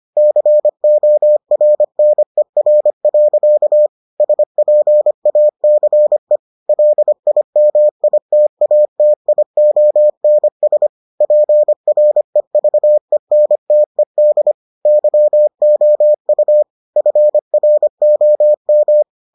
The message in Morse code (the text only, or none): CORNER. SPACE LIMITATIONS PREVENTED YOU FROM